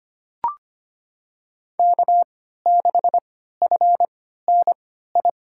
Morse code, heard as a keyed tone, about 25 wpm, spaced as for 14 wpm, 700 Hz. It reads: K6FNI
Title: USA Call Signs - 3x Character Spacing - 25wpm